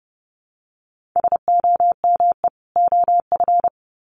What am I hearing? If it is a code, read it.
Morse code, heard as a keyed tone, about 30 wpm, 700 Hz. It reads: SOME OF
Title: Sets of 2 Words 30wpm